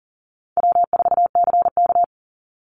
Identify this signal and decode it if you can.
Morse code, heard as a keyed tone, about 40 wpm, 700 Hz. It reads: W4CK
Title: Call Signs - 40wpm